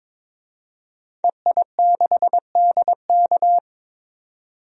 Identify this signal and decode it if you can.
Morse code, heard as a keyed tone, about 21 wpm, 700 Hz. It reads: EI6DK
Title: DXCC-100 Call Signs - 5 Characters - 22wpm